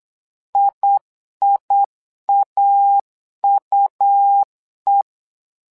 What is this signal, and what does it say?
Morse code, heard as a keyed tone, about 8 wpm, 800 Hz. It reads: IIAUE
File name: LEZIONI / 2 UHDN / 2 UH 8wpm 2129